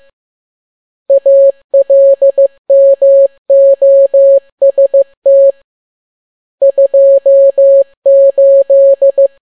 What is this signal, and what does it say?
Morse code, heard as a keyed tone, about 15 wpm, 550 Hz. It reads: ALMOST 28